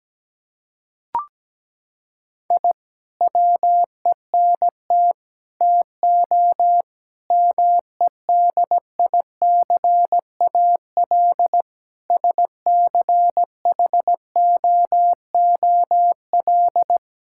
Morse code, heard as a keyed tone, about 17 wpm, 700 Hz. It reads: I WENT TO MEDICAL SCHOOL
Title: Sentences from Top 1000 Words - Review Part 5 - 17wpm